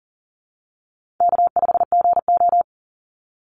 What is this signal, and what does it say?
Morse code, heard as a keyed tone, about 40 wpm, 700 Hz. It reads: K5GO
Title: Call Signs Repeated 3x - 40wpm